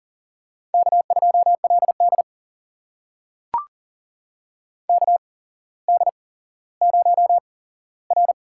Morse code, heard as a keyed tone, about 42 wpm, spaced as for 17 wpm, 700 Hz. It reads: K2LD KD0R
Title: USA Call Signs - 4 Characters - 8x Character Spacing - 40wpm